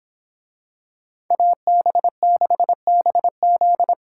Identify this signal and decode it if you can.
Morse code, heard as a keyed tone, about 24 wpm, 700 Hz. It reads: AB6BZ